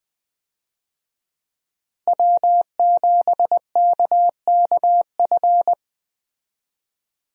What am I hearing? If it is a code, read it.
Morse code, heard as a keyed tone, about 20 wpm, 700 Hz. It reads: W7KKF
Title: USA Call Signs - 5 Characters - Repeated 2x - 20wpm